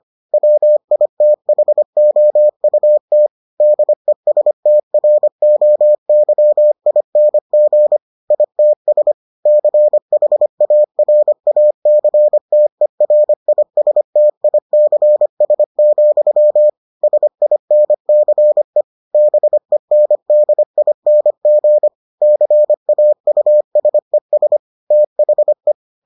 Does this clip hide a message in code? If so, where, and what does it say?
Yes: Morse code from 0.3 to 25.7 s, heard as a keyed tone, about 25 wpm, 600 Hz. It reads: WITHOUT DESTROYING ITS CHARACTERISTICS, SINCE BENDING CAUSES THE